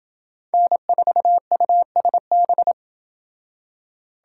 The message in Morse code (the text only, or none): N4USB